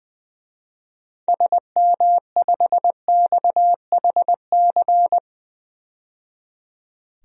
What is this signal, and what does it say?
Morse code, heard as a keyed tone, about 20 wpm, 700 Hz. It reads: SM5XHC
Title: DXCC-100 Call Signs - 6 Characters - 20wpm